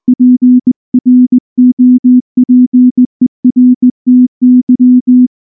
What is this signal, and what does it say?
Morse code, heard as a keyed tone, about 18 wpm, 250 Hz. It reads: PROPERTY
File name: morse / morse-training-20WPM-2